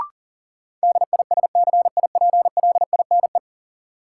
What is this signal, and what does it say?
Morse code, heard as a keyed tone, about 40 wpm, 700 Hz. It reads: DISCIPLINE
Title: Sentences from Top 900 Words - New Word Drill - 40wpm